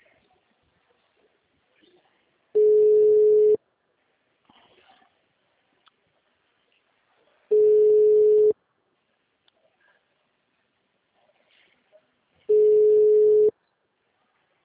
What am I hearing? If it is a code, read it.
Ringback tone (Europe, 425 Hz).